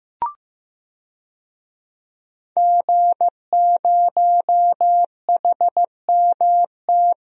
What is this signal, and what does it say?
Morse code, heard as a keyed tone, about 15 wpm, 700 Hz. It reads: G0HMT
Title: DXCC-100 Call Signs - 5 Characters - 15wpm